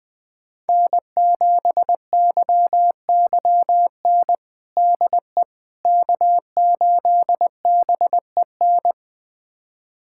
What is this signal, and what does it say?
Morse code, heard as a keyed tone, about 20 wpm, 700 Hz. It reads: N7YYN DE K8BEN
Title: USA Callsign DE Callsign - 5 Characters - 20wpm